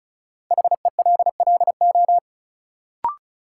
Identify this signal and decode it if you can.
Morse code, heard as a keyed tone, about 34 wpm, 700 Hz. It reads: HELLO
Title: Sentences from Top 1100 Words - New Word Drill - 35wpm